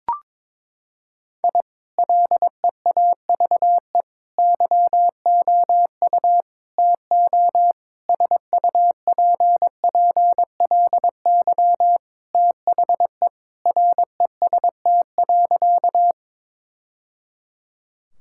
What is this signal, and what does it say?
Morse code, heard as a keyed tone, about 22 wpm, 700 Hz. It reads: I LEAVE YOU TO SUPPLY THE REST.
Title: Sentences from Top 1000 Words - Review Part 5 - 22wpm